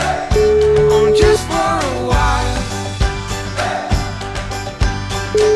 Ringback tone (Europe, 425 Hz), with noise.